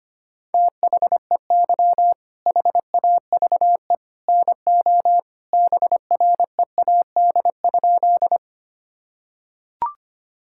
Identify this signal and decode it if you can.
Morse code, heard as a keyed tone, about 25 wpm, 700 Hz. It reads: THEY HAVE NO BREAD?
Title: Sentences from Top 1600 Words 25wpm